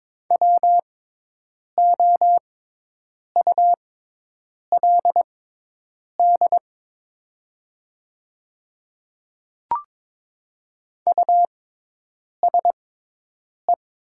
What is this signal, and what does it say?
Morse code, heard as a keyed tone, about 22 wpm, spaced as for 8 wpm, 700 Hz. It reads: WOULD USE
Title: 6x Character Spacing - Top 300 Words - 22wpm